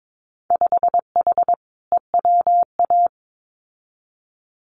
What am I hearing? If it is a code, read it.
Morse code, heard as a keyed tone, about 22 wpm, 700 Hz. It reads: 5H EWA
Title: Winter Field Day Exchanges Encore 22wpm